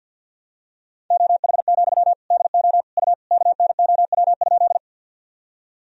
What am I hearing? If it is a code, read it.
Morse code, heard as a keyed tone, about 51 wpm, 700 Hz. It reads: OH, DO U KNOW?
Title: Sentences from Top 500 Words and Abbreviations 50wpm